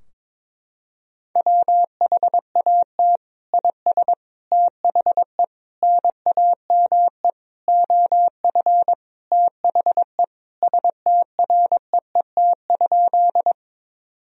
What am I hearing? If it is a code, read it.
Morse code, heard as a keyed tone, about 22 wpm, 700 Hz. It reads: WHAT IS THE NAME OF THE STREET?
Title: Sentences from Top 1000 Words - Review Part 3 - 22wpm